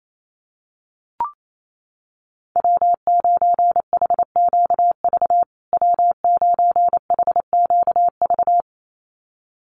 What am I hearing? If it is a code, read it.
Morse code, heard as a keyed tone, about 28 wpm, 700 Hz. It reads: W9HQV W9HQV
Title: USA Call Signs Repeated 2x 28wpm